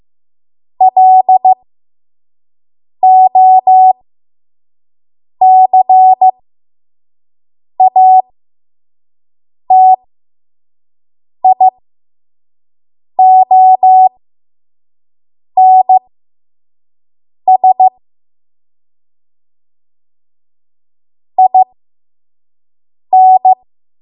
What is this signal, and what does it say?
Morse code, heard as a keyed tone, about 15 wpm, spaced as for 5 wpm, 750 Hz. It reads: LOCATIONS IN
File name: Morse / 05WPM